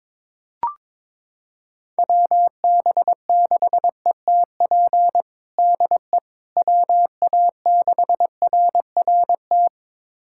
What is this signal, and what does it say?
Morse code, heard as a keyed tone, about 22 wpm, 700 Hz. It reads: WB6ETP DE WA6RRT
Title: USA Callsign DE Callsign - 6 Characters - 22wpm